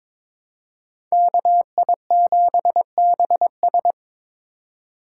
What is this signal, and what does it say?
Morse code, heard as a keyed tone, about 22 wpm, 700 Hz. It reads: KI7BS
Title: USA Call Signs - 5 Characters - 4x Character Spacing - 22wpm